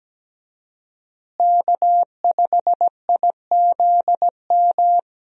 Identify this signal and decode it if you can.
Morse code, heard as a keyed tone, about 17 wpm, 700 Hz. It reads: K5IZM
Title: USA Call Signs - 5 Characters - Repeated 3x - 17wpm